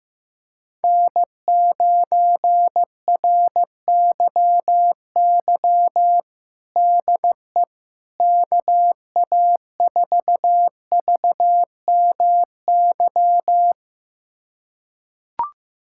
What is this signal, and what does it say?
Morse code, heard as a keyed tone, about 15 wpm, 700 Hz. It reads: N9RYY DE KA4VMY
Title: USA Callsign DE Callsign 15wpm